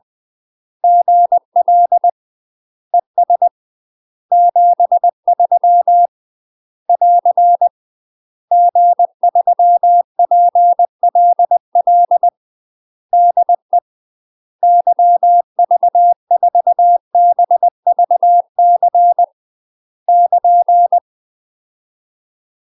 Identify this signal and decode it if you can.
Morse code, heard as a keyed tone, about 20 wpm, 700 Hz. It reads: GL ES 73 <AR> G3PLL DE YV4BVC <KN>
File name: UK MASTER QSOs TRAINING / 20WPM / UK MASTER QSOs 0024